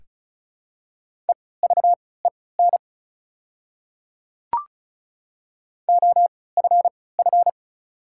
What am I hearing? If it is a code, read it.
Morse code, heard as a keyed tone, about 36 wpm, spaced as for 20 wpm, 700 Hz. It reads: EVEN OFF
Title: 3x Character Spacing - Top 200 Words - 35wpm